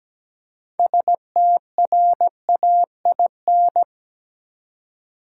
Morse code, heard as a keyed tone, about 17 wpm, 700 Hz. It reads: STRAIN
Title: Six Letter Words - 17wpm